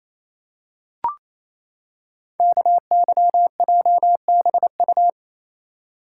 Morse code, heard as a keyed tone, about 27 wpm, 700 Hz. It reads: KYJBU
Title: Code Groups - 5 Letters - 28wpm